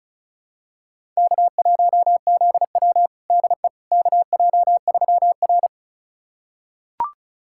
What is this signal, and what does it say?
Morse code, heard as a keyed tone, about 34 wpm, 700 Hz. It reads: K1ZW DE KJ3R
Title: USA Callsign DE Callsign - 4 Characters - 35wpm